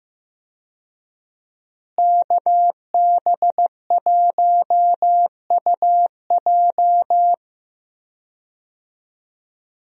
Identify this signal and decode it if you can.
Morse code, heard as a keyed tone, about 15 wpm, 700 Hz. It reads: KB1UJ
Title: USA Call Signs - 5 Characters - Repeated 3x - 15wpm